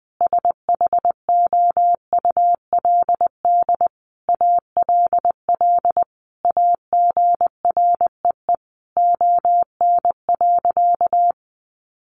Morse code, heard as a keyed tone, about 20 wpm, 700 Hz. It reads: SHOULD ALL AGREE ON.
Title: Sentences from Top 1100 Words 20wpm